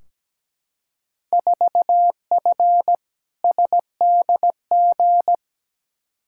Morse code, heard as a keyed tone, about 17 wpm, 700 Hz. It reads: 4F SDG
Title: Field Day Exchanges - 17wpm